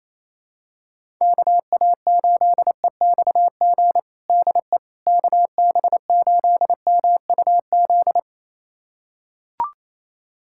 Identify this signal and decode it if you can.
Morse code, heard as a keyed tone, about 28 wpm, 700 Hz. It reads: KA8EXG DE KB8MUZ
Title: USA Callsign DE Callsign 28wpm